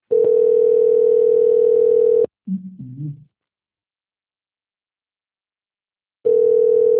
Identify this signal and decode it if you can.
Ringback tone (North America), 440 + 480 Hz.